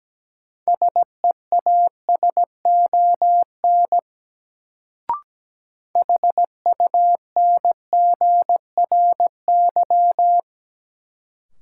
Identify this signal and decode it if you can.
Morse code, heard as a keyed tone, about 17 wpm, 700 Hz. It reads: SEASON HUNGRY
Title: Sentences from Top 1400 Words - New Word Drill Alt - 17wpm